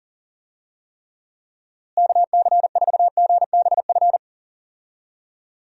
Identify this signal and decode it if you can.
Morse code, heard as a keyed tone, about 40 wpm, 700 Hz. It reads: KC4GBF
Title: USA Call Signs - 6 Characters - Repeated 3x - 40wpm